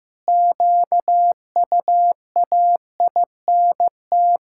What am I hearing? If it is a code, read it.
Morse code, heard as a keyed tone, about 15 wpm, 700 Hz. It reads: QUAINT
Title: Pangram Challenge v2 - 15wpm